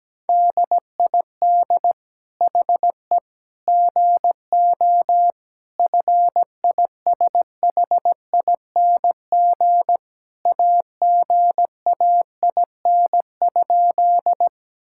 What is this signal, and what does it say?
Morse code, heard as a keyed tone, about 17 wpm, 700 Hz. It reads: DID HE GO FISHING AGAIN?
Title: Sentences from Top 1000 Words - Review Part 5 - 17wpm